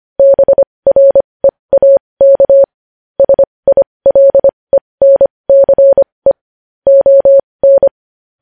Morse code, heard as a keyed tone, about 25 wpm, spaced as for 20 wpm, 550 Hz. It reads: BREAK SILENCE ON
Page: news-headlines-in-morse-code-at-20-wpm